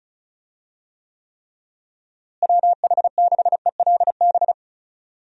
Morse code, heard as a keyed tone, about 36 wpm, 700 Hz. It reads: WH6ELB
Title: USA Call Signs - 6 Characters - Repeated 3x - 35wpm